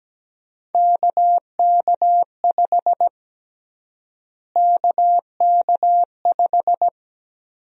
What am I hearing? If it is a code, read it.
Morse code, heard as a keyed tone, about 17 wpm, 700 Hz. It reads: KK5 KK5
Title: USA Call Sign Prefixes - Repeated 2x - 17wpm